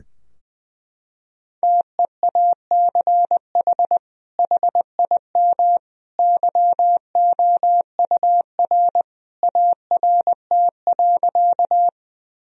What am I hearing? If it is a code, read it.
Morse code, heard as a keyed tone, about 20 wpm, 700 Hz. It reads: TEACH HIM YOUR ART.
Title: Sentences from Top 1000 Words - Review Part 3 - 20wpm